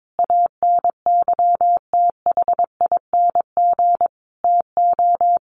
Morse code, heard as a keyed tone, about 22 wpm, 700 Hz. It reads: ANYTHING TO
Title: Sentences from Top 1000 Words 22wpm